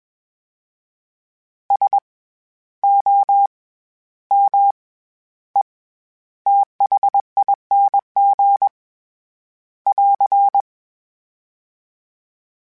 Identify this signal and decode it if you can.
Morse code, heard as a keyed tone, about 21 wpm, 800 Hz. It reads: S O M E THING <AR>
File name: Special INGs-mp3s / Special INGs-mp3s / SlowToFaster ing @20